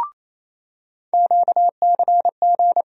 Morse code, heard as a keyed tone, about 28 wpm, 700 Hz. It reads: QCG
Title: USA Call Sign Postfixes - 3 Letters 28wpm